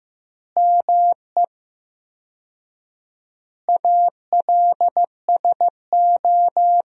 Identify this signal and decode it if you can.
Morse code, heard as a keyed tone, about 15 wpm, 700 Hz. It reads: ME ALSO
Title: 4x Word-Spacing - Sentences from Top 400 Words - 15wpm